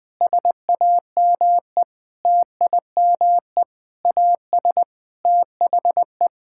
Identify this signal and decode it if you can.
Morse code, heard as a keyed tone, about 20 wpm, 700 Hz. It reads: SAME TIME AS THE
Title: Sentences with 6-Word Phrases - 20wpm